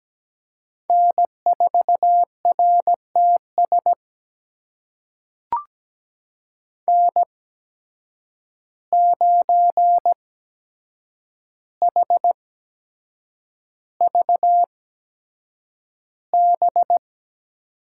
Morse code, heard as a keyed tone, about 17 wpm, spaced as for 7 wpm, 700 Hz. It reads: N4RTS N9HVB
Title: USA Call Signs - 5 Characters - 8x Character Spacing - 17wpm